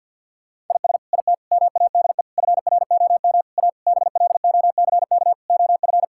Morse code, heard as a keyed tone, about 50 wpm, 700 Hz. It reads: IS IT MADE FROM A BLOCK OF
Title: Sentences from Top 1600 Words 50wpm